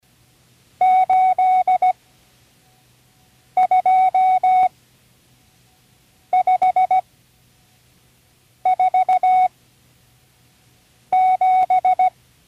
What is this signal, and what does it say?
Morse code, heard as a keyed tone, about 17 wpm, spaced as for 5 wpm, 740 Hz. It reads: 82547
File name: CW Learning Programs / Learning Morse Code (from Tune in the World) / 37 all numbers